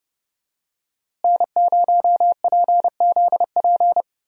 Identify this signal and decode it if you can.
Morse code, heard as a keyed tone, about 30 wpm, 700 Hz. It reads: N0PZP
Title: USA Call Signs - 5 Characters - Repeated 3x - 30wpm